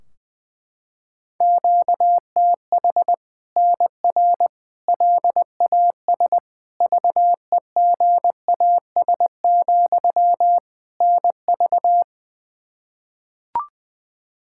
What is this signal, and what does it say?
Morse code, heard as a keyed tone, about 20 wpm, 700 Hz. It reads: QTH NR LAS VEGAS, NV
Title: Practice QSO - Exchange 1 - Beginning - 20wpm